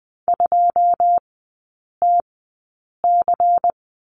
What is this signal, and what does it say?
Morse code, heard as a keyed tone, about 19 wpm, spaced as for 8 wpm, 700 Hz. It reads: 2TC
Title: 127 Lesson 21e - Number 6 - Three Characters 20wpm